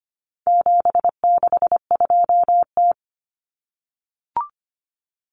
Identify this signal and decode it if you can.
Morse code, heard as a keyed tone, about 25 wpm, 700 Hz. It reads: 762T
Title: Common Cut Numbers - 4-Digit 25wpm